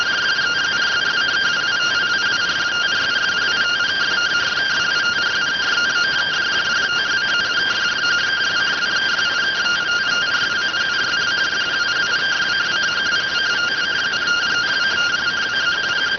Two-tone teleprinter signal, about 170 Hz shift.